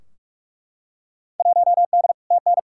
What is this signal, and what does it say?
Morse code, heard as a keyed tone, about 43 wpm, 700 Hz. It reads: 1D TN
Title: Field Day Exchanges - 45wpm